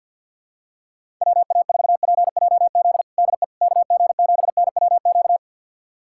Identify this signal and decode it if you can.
Morse code, heard as a keyed tone, about 52 wpm, 700 Hz. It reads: WA4PJZ DE KG7NWQ
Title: USA Callsign DE Callsign 50wpm